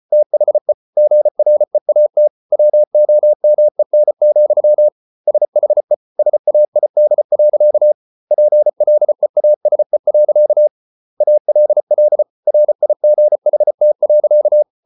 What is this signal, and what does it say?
Morse code, heard as a keyed tone, about 34 wpm, 600 Hz. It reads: THE GREAT WOMEN, SHE SAID. PLEASE. ALL RIGHT.